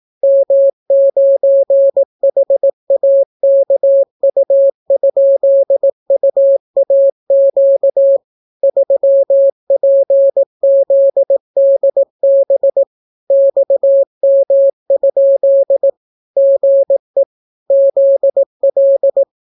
Morse code, heard as a keyed tone, about 18 wpm, 550 Hz. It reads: M9HAKU?UAQ 3PZDB XM? GE ZL